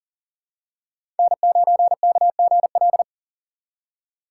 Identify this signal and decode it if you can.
Morse code, heard as a keyed tone, about 40 wpm, 700 Hz. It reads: N9KGL
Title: USA Call Signs - 5 Characters - 6x Character Spacing - 40wpm